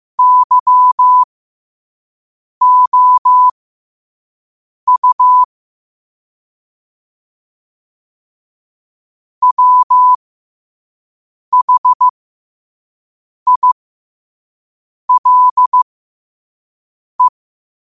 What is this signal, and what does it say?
Morse code, heard as a keyed tone, about 15 wpm, spaced as for 5 wpm, 1000 Hz. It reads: YOU WHILE
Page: New quotes every day in morse code at 5 Words per minute.